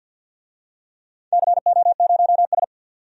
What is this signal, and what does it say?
Morse code, heard as a keyed tone, about 46 wpm, 700 Hz. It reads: CO0S
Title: DXCC-100 Call Signs - 4 Characters - 50wpm